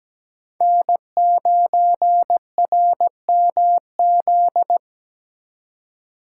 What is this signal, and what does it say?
Morse code, heard as a keyed tone, about 20 wpm, 700 Hz. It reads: N9RMZ
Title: Speed Racing - USA 1x3 Call Signs 17wpm